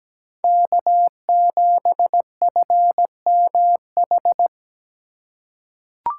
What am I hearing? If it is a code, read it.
Morse code, heard as a keyed tone, about 17 wpm, 700 Hz. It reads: K7FMH
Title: USA Call Signs - 5 Characters - 3x Character Spacing - 17wpm